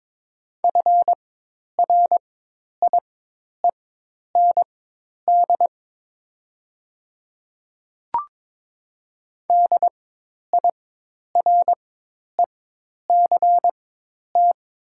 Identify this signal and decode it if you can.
Morse code, heard as a keyed tone, about 22 wpm, spaced as for 10 wpm, 700 Hz. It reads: FRIEND DIRECT
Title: 4x Character Spacing - Top 500 Words - 22wpm